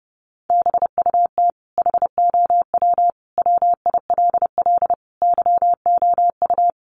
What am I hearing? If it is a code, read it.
Morse code, heard as a keyed tone, about 30 wpm, 700 Hz. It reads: BUT HOW WILL YOU
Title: Sentences from Top 1100 Words 30wpm